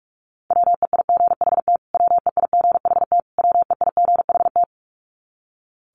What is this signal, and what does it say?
Morse code, heard as a keyed tone, about 44 wpm, 700 Hz. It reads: WEIGHT WEIGHT WEIGHT
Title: Top 500 Words Repeated 45wpm